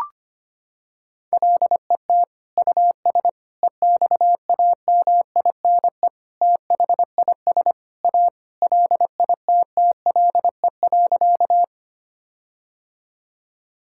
Morse code, heard as a keyed tone, about 25 wpm, 700 Hz. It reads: LET US EXAMINE THIS A LITTLE.
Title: Sentences from Top 1600 Words 25wpm